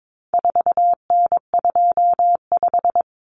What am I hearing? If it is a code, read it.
Morse code, heard as a keyed tone, about 22 wpm, 700 Hz. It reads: 4N25
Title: Common Cut Numbers - 4-Digit 22wpm